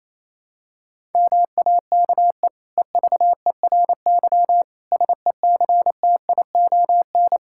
Morse code, heard as a keyed tone, about 28 wpm, 700 Hz. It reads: MAKE EVERY SECTION
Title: Sentences from Top 1000 Words - Review Part 5 - 28wpm